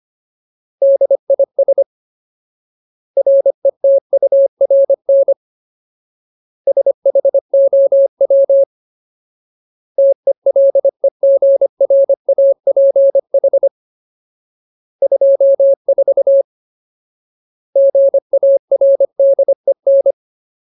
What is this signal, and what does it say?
Morse code, heard as a keyed tone, about 25 wpm, 550 Hz. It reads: DIS RETURN SHOW TELEGRAPH 24 GARDEN
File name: Ultimate Word List / 25WPM Ultimate